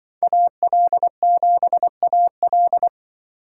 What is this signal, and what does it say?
Morse code, heard as a keyed tone, about 24 wpm, 700 Hz. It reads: AL7AL
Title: Speed Racing - USA 2x2 Call Signs 20wpm